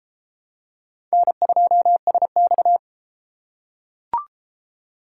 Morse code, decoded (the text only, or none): N2SX